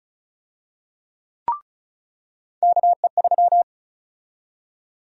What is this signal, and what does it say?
Morse code, heard as a keyed tone, about 34 wpm, 700 Hz. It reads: KE3
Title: USA Call Sign Prefixes - 35wpm